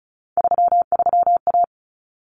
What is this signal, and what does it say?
Morse code, heard as a keyed tone, about 34 wpm, 700 Hz. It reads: 33A
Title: Common Cut Numbers - 3-Digit 35wpm